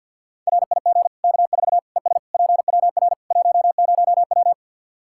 Morse code, heard as a keyed tone, about 52 wpm, 700 Hz. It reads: RIG K4 ES PWR 10W
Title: Practice QSO - Exchange 2 - Info - 50wpm